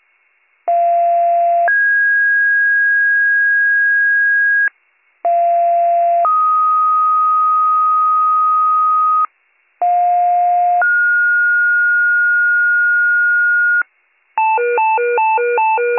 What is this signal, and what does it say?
Two-tone page, heard as a one-second tone, then a three-second tone, about 690 then 1650 Hz, about 690 then 1250 Hz, about 710 then 1430 Hz, with noise. It is heard from 0.7 to 13.8 s.